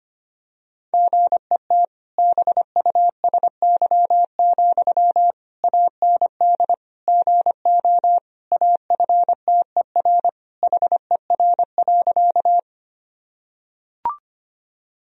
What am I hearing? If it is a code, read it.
Morse code, heard as a keyed tone, about 25 wpm, 700 Hz. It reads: GET BUSY, AND GO AFTER HER.
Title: Sentences from Top 1000 Words - Review Part 3 - 25wpm